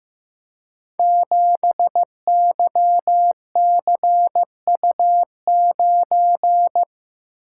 Morse code, heard as a keyed tone, about 15 wpm, 700 Hz. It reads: 7YCU9
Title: Code Groups - 5 Letter:Numbers - 15wpm